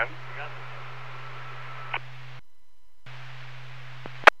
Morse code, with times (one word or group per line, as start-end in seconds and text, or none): none